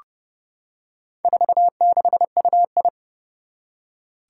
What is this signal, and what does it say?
Morse code, heard as a keyed tone, about 32 wpm, 700 Hz. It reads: 46UI
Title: Four Challenging Letter-Number Combinations 30wpm